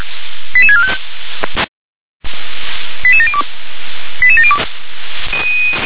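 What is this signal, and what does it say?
Five-tone selective call (ZVEI), heard as quick runs of five tones, tones of about 70 ms. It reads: 88645 88702 88702
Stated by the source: ZVEI five-tone selective call received on an SDR.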